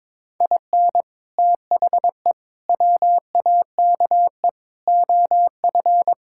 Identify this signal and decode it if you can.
Morse code, heard as a keyed tone, about 22 wpm, 700 Hz. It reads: IN THE WAKE OF
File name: practice / 22wpm / Build A Phrase - 5-Word Phrases - 22wpm